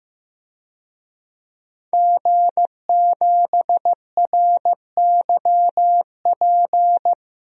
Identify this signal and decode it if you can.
Morse code, heard as a keyed tone, about 15 wpm, 700 Hz. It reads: G7RYP
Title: DXCC-100 Call Signs - 5 Characters - 15wpm